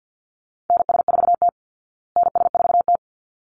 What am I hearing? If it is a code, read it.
Morse code, heard as a keyed tone, about 50 wpm, 700 Hz. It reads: NS4T NS4T
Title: Call Signs Repeated 3x - 50wpm